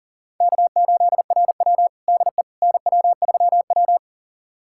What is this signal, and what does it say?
Morse code, heard as a keyed tone, about 40 wpm, 700 Hz. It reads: K8RW DE NW3W
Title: USA Callsign DE Callsign - 4 Characters - 40wpm